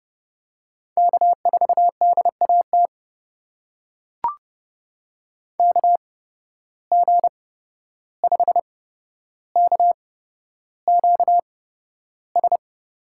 Morse code, heard as a keyed tone, about 30 wpm, spaced as for 12 wpm, 700 Hz. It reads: K4DAT KG5KQS
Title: USA Call Signs - 8x Character Spacing - 30wpm